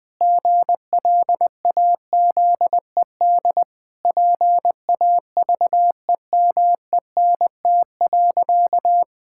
Morse code, heard as a keyed tone, about 20 wpm, 700 Hz. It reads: GLAZED PAVEMENT.
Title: Pangram Challenge v2 - 20wpm